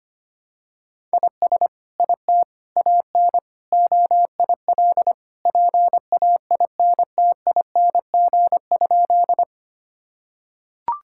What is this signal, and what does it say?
Morse code, heard as a keyed tone, about 25 wpm, 700 Hz. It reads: IS IT AN OIL PAINTING?
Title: Sentences from Top 1000 Words - Review Part 6 - 25wpm